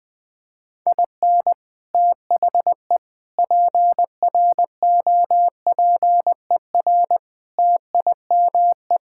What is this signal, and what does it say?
Morse code, heard as a keyed tone, about 21 wpm, 700 Hz. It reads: IN THE PROPER TIME
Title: Sentences from Top 1000 Words - Review Part 5 - 20wpm